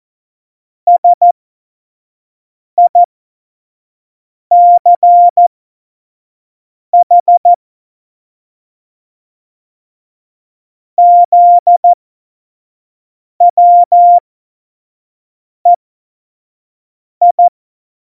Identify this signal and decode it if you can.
Morse code, heard as a keyed tone, about 14 wpm, spaced as for 5 wpm, 700 Hz. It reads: SICH ZWEI
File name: Morsekurs / Witze / Witz 03.Juli 2018, 25 F70